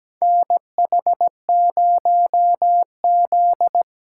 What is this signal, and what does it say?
Morse code, heard as a keyed tone, about 17 wpm, 700 Hz. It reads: NH0Z
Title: USA Call Signs - 4 Characters - 3x Character Spacing - 17wpm